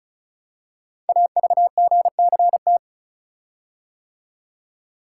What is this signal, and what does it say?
Morse code, heard as a keyed tone, about 36 wpm, 700 Hz. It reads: AVGCT
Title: Code Groups - 5 Letter:Numbers - 35wpm